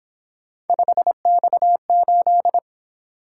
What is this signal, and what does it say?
Morse code, heard as a keyed tone, about 27 wpm, 700 Hz. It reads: 5X8
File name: Speed-Racing - Three Letter-Number Combinations - 22wpm